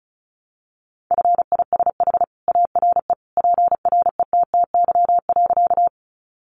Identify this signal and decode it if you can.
Morse code, heard as a keyed tone, about 34 wpm, 700 Hz. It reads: FISH ARE PRETTY.
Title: Additional Easy Phrases 35wpm